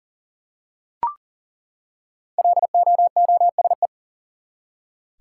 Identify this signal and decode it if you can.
Morse code, heard as a keyed tone, about 40 wpm, 700 Hz. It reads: LOOSE
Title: Five Letter Words - 40wpm